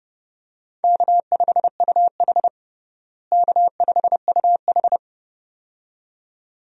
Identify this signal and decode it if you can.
Morse code, heard as a keyed tone, about 30 wpm, 700 Hz. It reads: K5UH K5UH
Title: USA Call Signs - 4 Characters - Repeated 2x - 30wpm